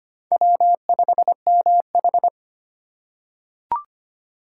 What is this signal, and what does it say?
Morse code, heard as a keyed tone, about 25 wpm, 700 Hz. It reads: W5MH
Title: USA Call Signs - 4 Characters - 4x Character Spacing - 25wpm